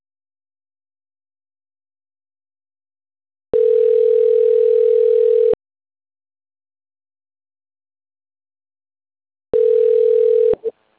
Ringback tone (North America), 440 + 480 Hz.
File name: autotestagent / 11-22-28.136..callcentertransfer. .7bd6454d-4aee-419b-949f